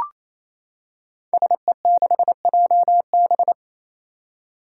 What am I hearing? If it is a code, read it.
Morse code, heard as a keyed tone, about 28 wpm, 700 Hz. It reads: SE6JB
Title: Five Challenging Letter-Number Combinations 28wpm